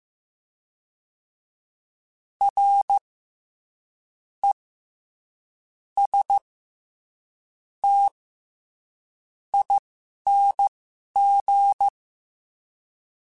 Morse code, heard as a keyed tone, about 15 wpm, spaced as for 6 wpm, 800 Hz. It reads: RESTING
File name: Special INGs-mp3s / Special INGs-mp3s / SlowToFaster ing @10